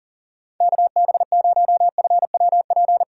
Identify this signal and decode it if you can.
Morse code, heard as a keyed tone, about 39 wpm, 700 Hz. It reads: KB0FWP
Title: USA Call Signs - 6 Characters - 4x Character Spacing - 40wpm